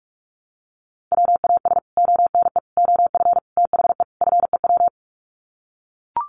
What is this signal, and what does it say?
Morse code, heard as a keyed tone, about 44 wpm, 700 Hz. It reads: WAS ONE OF THE FEW
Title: Sets of 5 Words Encore 45wpm